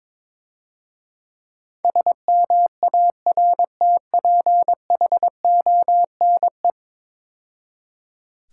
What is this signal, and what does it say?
Morse code, heard as a keyed tone, about 22 wpm, 690 Hz. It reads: SMARTPHONE
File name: Easy Words - Speed-Racing with Hints - 22wpm